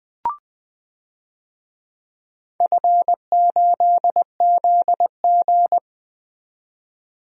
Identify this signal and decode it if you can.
Morse code, heard as a keyed tone, about 21 wpm, 700 Hz. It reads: F8ZG
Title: DXCC-100 Call Signs - 4 Characters - 20wpm